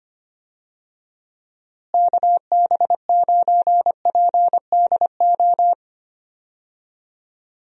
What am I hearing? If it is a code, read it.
Morse code, heard as a keyed tone, about 25 wpm, 700 Hz. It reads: KB9PDO
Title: USA Call Signs - 6 Characters - Repeated 3x - 25wpm